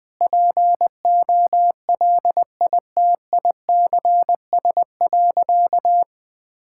Morse code, heard as a keyed tone, about 21 wpm, 700 Hz. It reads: POLITICS.
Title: Sentences from Top 1000 Words - Review Part 6 - 20wpm